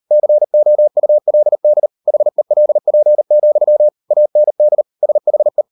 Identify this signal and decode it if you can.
Morse code, heard as a keyed tone, about 38 wpm, 600 Hz. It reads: COULD HELP, AND SHE